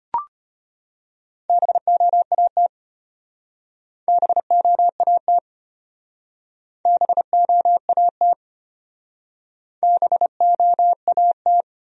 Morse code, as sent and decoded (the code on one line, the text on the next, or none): -... --- .- - / -... --- .- - / -... --- .- - / -... --- .- -
BOAT BOAT BOAT BOAT